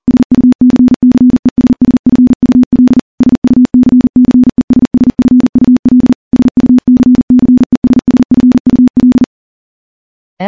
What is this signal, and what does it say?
Morse code, heard as a keyed tone, about 40 wpm, 250 Hz. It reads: SUCCESSFUL SUCCESSFUL SUCCESSFUL
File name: morse-training-40WPM